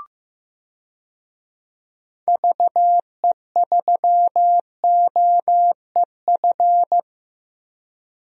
Morse code, heard as a keyed tone, about 15 wpm, 700 Hz. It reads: VE3OEF
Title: DXCC-100 Call Signs - 6 Characters - 15wpm